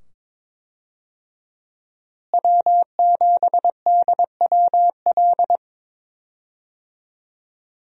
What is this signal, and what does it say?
Morse code, heard as a keyed tone, about 22 wpm, 700 Hz. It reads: W7DWL
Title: USA Call Signs - 5 Characters - Repeated 2x - 22wpm